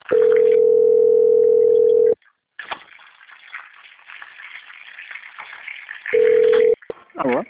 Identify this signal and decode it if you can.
Ringback tone (North America), 440 + 480 Hz.
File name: out- - - - - .1560